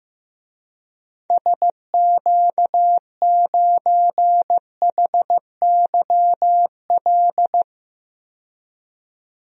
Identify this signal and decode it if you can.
Morse code, heard as a keyed tone, about 15 wpm, 690 Hz. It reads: SQ9HYL
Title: DXCC-100 Call Signs - 6 Characters - 15wpm